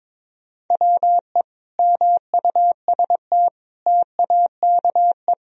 Morse code, heard as a keyed tone, about 22 wpm, 700 Hz. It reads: WE MUST TAKE
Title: Sentences from Top 1000 Words - Review Part 1 - 22wpm